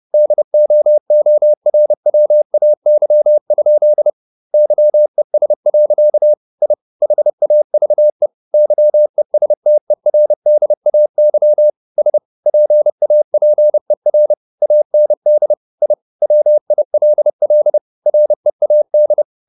Morse code, heard as a keyed tone, about 30 wpm, 600 Hz. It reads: DOORWAY? YES. I HAVE YESTERDAY S PAPER AND I WILL READ
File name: the old man and the sea / the old man and the sea 1 30wpm